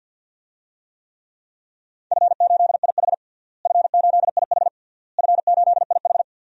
Morse code, heard as a keyed tone, about 46 wpm, 700 Hz. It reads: F8IH F8IH F8IH
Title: DXCC-100 Call Signs - 4 Characters - Repeated 3x - 50wpm